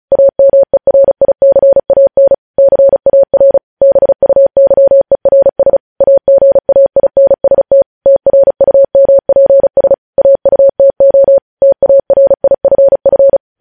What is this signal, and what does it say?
Morse code, heard as a keyed tone, about 36 wpm, 550 Hz. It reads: AMERICAN CAR BUYERS AGAINST TRUMPS AUTO TARIFF
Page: News headlines in Morse code at 35 wpm